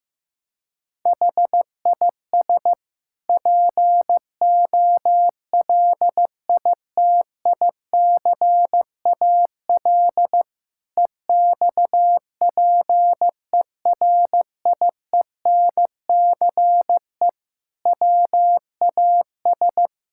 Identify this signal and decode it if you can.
Morse code, heard as a keyed tone, about 15 wpm, 700 Hz. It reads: HIS POLITICAL EXPERIENCE WAS
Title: Sentences from Top 1000 Words - Review Part 6 - 15wpm